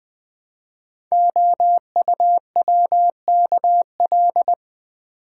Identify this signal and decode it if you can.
Morse code, heard as a keyed tone, about 20 wpm, 700 Hz. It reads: OUWKL
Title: Code Groups - 5 Letters - 20wpm